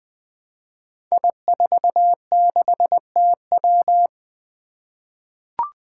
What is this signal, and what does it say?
Morse code, heard as a keyed tone, about 20 wpm, 700 Hz. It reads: I46TW
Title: Five Challenging Letter-Number Combinations 20wpm